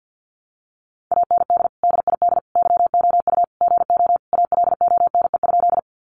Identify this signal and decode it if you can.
Morse code, heard as a keyed tone, about 50 wpm, 700 Hz. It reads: AND DID YOU GO ALONE?
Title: Sentences from Top 1000 Words 50wpm